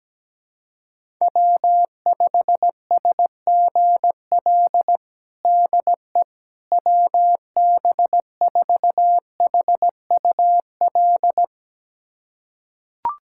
Morse code, heard as a keyed tone, about 17 wpm, 700 Hz. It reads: W5SGL DE WB4HUL
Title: USA Callsign DE Callsign 17wpm